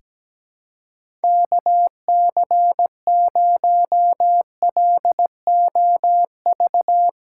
Morse code, heard as a keyed tone, about 17 wpm, 700 Hz. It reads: KC0LOV
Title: USA Call Signs - 6 Characters - 4x Character Spacing - 17wpm